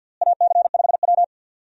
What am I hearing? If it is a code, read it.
Morse code, heard as a keyed tone, about 52 wpm, 700 Hz. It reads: AC5W
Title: USA Call Signs - 4 Characters - Repeated 2x - 50wpm